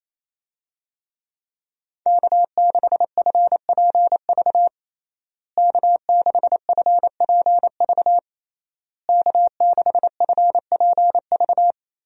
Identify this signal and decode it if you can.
Morse code, heard as a keyed tone, about 28 wpm, 700 Hz. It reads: K6FPV K6FPV K6FPV
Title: USA Call Signs - 5 Characters - Repeated 3x - 28wpm